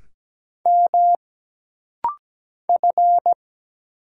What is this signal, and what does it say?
Morse code, heard as a keyed tone, about 17 wpm, 700 Hz. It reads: M F
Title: Single Letter-Number Alt - Rapid-Fire 17wpm